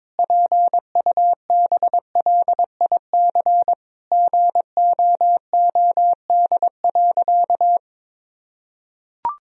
Morse code, heard as a keyed tone, about 22 wpm, 700 Hz. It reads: PUBLIC GOOD.
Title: Sentences from Top 1000 Words - Review Part 3 - 22wpm